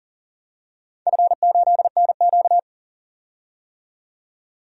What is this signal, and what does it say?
Morse code, heard as a keyed tone, about 40 wpm, 700 Hz. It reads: F8NQ
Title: DXCC-100 Call Signs - 4 Characters - 40wpm